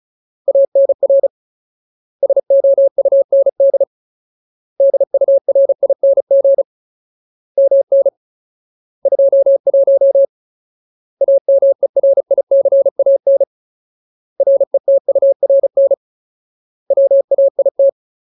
Morse code, heard as a keyed tone, about 36 wpm, 550 Hz. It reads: ANR SOUND DURING MN 21 AMERICAN RETURN WAIT
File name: Ultimate Word List / 35WPM Ultimate